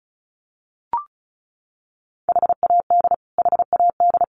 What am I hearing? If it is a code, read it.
Morse code, heard as a keyed tone, about 34 wpm, 700 Hz. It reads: HAD HAD
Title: Top 500 Words Repeated 35wpm